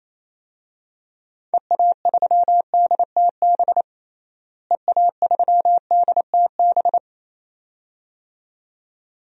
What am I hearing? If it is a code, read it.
Morse code, heard as a keyed tone, about 28 wpm, 700 Hz. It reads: EA3DTB EA3DTB
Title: DXCC-100 Call Signs - 6 Characters - Repeated 2x - 28wpm